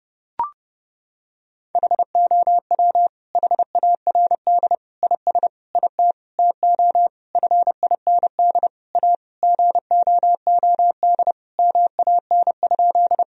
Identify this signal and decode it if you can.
Morse code, heard as a keyed tone, about 30 wpm, 700 Hz. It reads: HOW HARD IS IT TO FIND A GOOD MAN?
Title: Sentences from Top 1000 Words - Review Part 1 - 30wpm